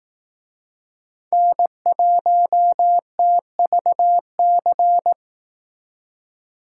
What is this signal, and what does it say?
Morse code, heard as a keyed tone, about 18 wpm, 700 Hz. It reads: N1TVC
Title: Speed Racing - USA 1x3 Call Signs 15wpm